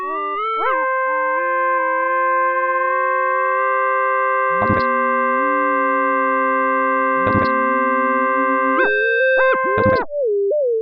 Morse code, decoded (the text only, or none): none